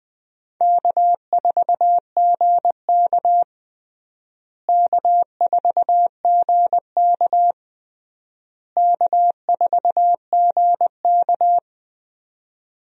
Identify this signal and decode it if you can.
Morse code, heard as a keyed tone, about 20 wpm, 700 Hz. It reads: K4GK K4GK K4GK
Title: USA Call Signs - 4 Characters - Repeated 3x - 20wpm